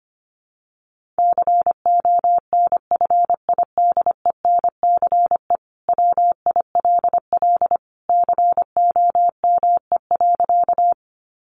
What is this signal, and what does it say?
Morse code, heard as a keyed tone, about 25 wpm, 700 Hz. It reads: CONFIDENCE WILL COME.
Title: Sentences from Top 1500 Words 25wpm